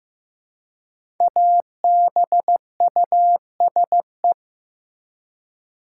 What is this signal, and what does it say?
Morse code, heard as a keyed tone, about 15 wpm, 700 Hz. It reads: ABUSE
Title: Five Letter Words - 15wpm